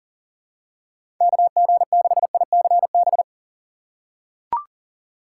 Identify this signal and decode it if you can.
Morse code, heard as a keyed tone, about 40 wpm, 700 Hz. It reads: KG6ICB
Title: USA Call Signs - 6x Character Spacing - 40wpm